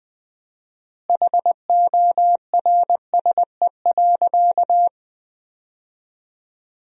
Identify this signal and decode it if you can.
Morse code, heard as a keyed tone, about 20 wpm, 700 Hz. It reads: HORSE.
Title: 4x Word-Spacing - Sentences from Top 500 Words - 20wpm